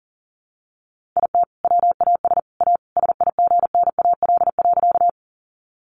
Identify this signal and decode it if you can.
Morse code, heard as a keyed tone, about 40 wpm, 700 Hz. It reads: IT WAS A SIGNAL.
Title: Sentences from Top 1400 Words 40wpm